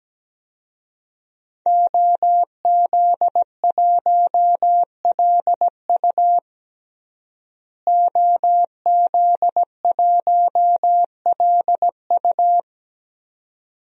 Morse code, heard as a keyed tone, about 17 wpm, 700 Hz. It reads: OZ1LU OZ1LU
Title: DXCC-100 Call Signs - 5 Characters - Repeated 2x - 17wpm